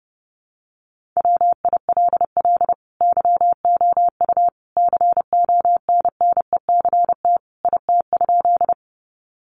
Morse code, heard as a keyed tone, about 28 wpm, 700 Hz. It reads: WILL YOU CONNECT IT?
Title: Sentences from Top 1500 Words 30wpm